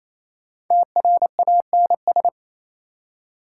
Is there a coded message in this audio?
Yes: Morse code from 0.7 to 2.3 s, heard as a keyed tone, about 37 wpm, 700 Hz. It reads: TRANS